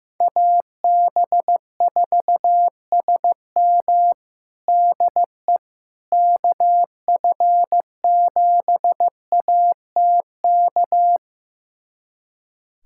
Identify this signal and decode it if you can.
Morse code, heard as a keyed tone, about 15 wpm, 700 Hz. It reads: AB4SM DE KF7ATK
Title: USA Callsign DE Callsign 15wpm